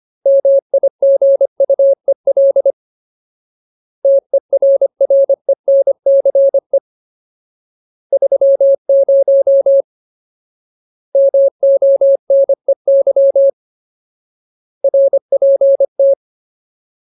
Morse code, heard as a keyed tone, about 25 wpm, 550 Hz. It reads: MIGUEL TERRENCE 30 MONEY RPT
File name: Ultimate Word List / 25WPM Ultimate